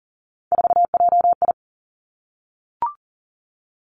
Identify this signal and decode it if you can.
Morse code, heard as a keyed tone, about 40 wpm, 700 Hz. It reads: 4JI
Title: Three Challenging Letter-Number Combinations 40wpm